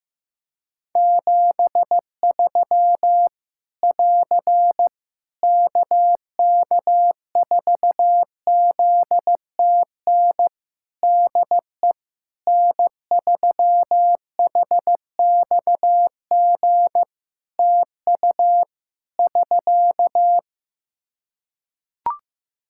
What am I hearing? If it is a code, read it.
Morse code, heard as a keyed tone, about 15 wpm, 700 Hz. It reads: 73 <AR> KK4ZTN DE N3HXG TU <SK>
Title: Practice QSO - Exchange 4 - Ending - 15wpm